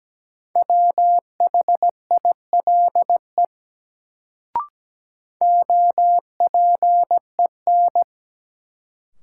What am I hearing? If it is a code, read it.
Morse code, heard as a keyed tone, about 17 wpm, 700 Hz. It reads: WHILE OPEN
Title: Sentences from Top 300 Words - New Word Drill - 17wpm